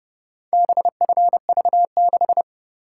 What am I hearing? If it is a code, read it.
Morse code, heard as a keyed tone, about 28 wpm, 700 Hz. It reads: BFV6